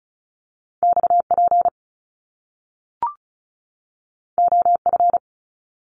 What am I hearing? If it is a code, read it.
Morse code, heard as a keyed tone, about 35 wpm, 700 Hz. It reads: XP OF
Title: Top Two Letter Combinations in Words 35wpm